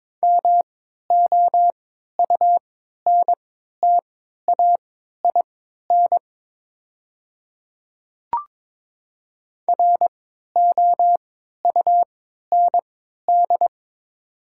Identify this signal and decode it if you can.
Morse code, heard as a keyed tone, about 21 wpm, spaced as for 12 wpm, 700 Hz. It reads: MOUNTAIN ROUND
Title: 3x Character Spacing - Top 300 Words - 22wpm